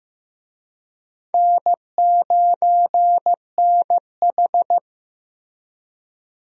Morse code, heard as a keyed tone, about 15 wpm, 700 Hz. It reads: N9NH
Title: USA Call Signs - 4 Characters - Repeated 3x - 15wpm